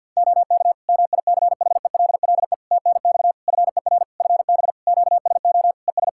Morse code, heard as a keyed tone, about 51 wpm, 700 Hz. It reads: OK MICHELLE TNX FER FB QSO ES
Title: Practice QSO - Exchange 4 - Ending - 50wpm